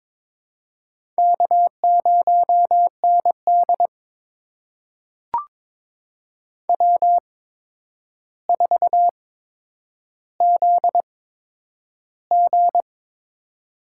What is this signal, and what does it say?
Morse code, heard as a keyed tone, about 22 wpm, spaced as for 9 wpm, 700 Hz. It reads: K0ND W4ZG
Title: USA Call Signs - 4 Characters - 8x Character Spacing - 22wpm